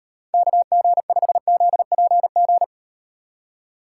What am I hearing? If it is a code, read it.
Morse code, heard as a keyed tone, about 41 wpm, 700 Hz. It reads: KG5ZPG